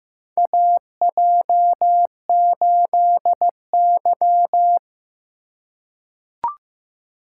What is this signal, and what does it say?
Morse code, heard as a keyed tone, about 15 wpm, 700 Hz. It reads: AJ8Y
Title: USA Call Signs - 2x Character Spacing - 15wpm